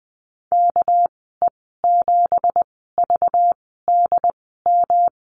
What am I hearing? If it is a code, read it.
Morse code, heard as a keyed tone, about 20 wpm, 700 Hz. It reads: K E 7 V D M
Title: 200 Lesson 32g - Letter k - Call Signs 20wpm